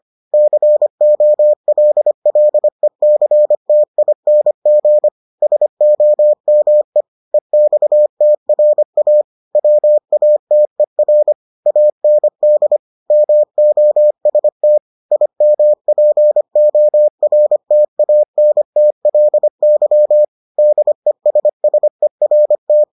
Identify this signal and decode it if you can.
Morse code, heard as a keyed tone, about 25 wpm, 600 Hz. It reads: COLLECTING SOME EXTRA WATER AND MOST IMPORTANTLY DESSERT